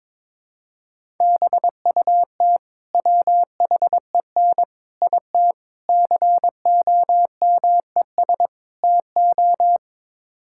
Morse code, heard as a keyed tone, about 22 wpm, 700 Hz. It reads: BUT WHEN IT COMES TO
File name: practice / 22wpm / Build A Phrase - 5-Word Phrases - 22wpm